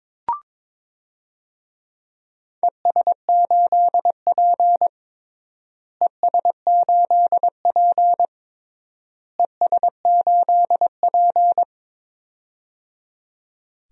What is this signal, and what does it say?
Morse code, heard as a keyed tone, about 22 wpm, 700 Hz. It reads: ES8P ES8P ES8P
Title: DXCC-100 Call Signs - 4 Characters - Repeated 3x - 22wpm